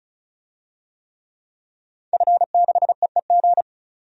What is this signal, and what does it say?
Morse code, heard as a keyed tone, about 34 wpm, 700 Hz. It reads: F6EEG
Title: DXCC-100 Call Signs - 5 Characters - 35wpm